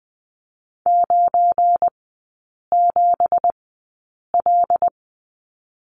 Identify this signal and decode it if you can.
Morse code, heard as a keyed tone, about 20 wpm, spaced as for 8 wpm, 700 Hz. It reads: 97L
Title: 178 Lesson 29e - Slash - Three Characters 20wpm